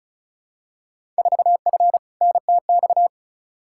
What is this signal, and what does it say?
Morse code, heard as a keyed tone, about 35 wpm, 700 Hz. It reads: 4F NTX
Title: Field Day Exchanges Encore - 35wpm